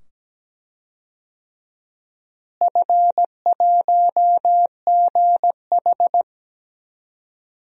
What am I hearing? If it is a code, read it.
Morse code, heard as a keyed tone, about 17 wpm, 700 Hz. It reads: F1GH
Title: DXCC-100 Call Signs - 4 Characters - Repeated 2x - 17wpm